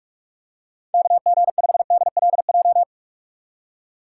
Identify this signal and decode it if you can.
Morse code, heard as a keyed tone, about 46 wpm, 700 Hz. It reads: KG5DLJ
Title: USA Call Signs - 6 Characters - 6x Character Spacing - 45wpm